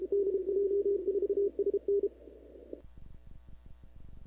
Morse code, heard as a keyed tone, about 32 wpm, 380 Hz. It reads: DJ4SN
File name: CW / 2005-k5tr arrl cw dx -15:00